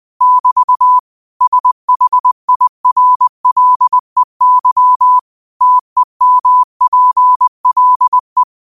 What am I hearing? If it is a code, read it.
Morse code, heard as a keyed tone, about 20 wpm, 1000 Hz. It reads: <BT> SHIRLEY TEMPLE
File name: qotdfiles / 20wpm